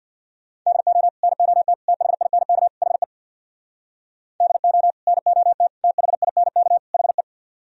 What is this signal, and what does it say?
Morse code, heard as a keyed tone, about 64 wpm, 700 Hz. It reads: DO NOT THINK HE DO NOT THINK HE